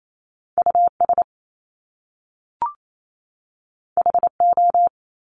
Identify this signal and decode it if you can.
Morse code, heard as a keyed tone, about 28 wpm, 700 Hz. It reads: US HO
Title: Top Two Letter Combinations in Words 28wpm